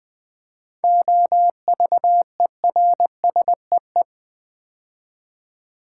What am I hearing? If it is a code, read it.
Morse code, heard as a keyed tone, about 20 wpm, 700 Hz. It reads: OVERSEE
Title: 7 Letter Words - First 3 Letters Given - 20wpm